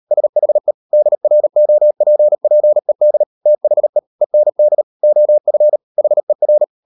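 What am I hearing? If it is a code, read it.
Morse code, heard as a keyed tone, about 38 wpm, 600 Hz. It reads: SHE DROPPED THE END OF HER
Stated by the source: Morse code practice at 38 WPM, sending Ken Follett's novel Never in English, from CW med Gnister.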